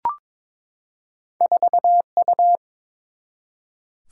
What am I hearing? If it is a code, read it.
Morse code, heard as a keyed tone, about 23 wpm, 700 Hz. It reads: 4U
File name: practice / 22wpm / Two Letter-Number Combinations - 22wpm